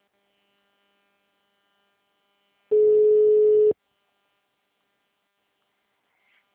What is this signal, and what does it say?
Ringback tone (Europe, 425 Hz).
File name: out- -2031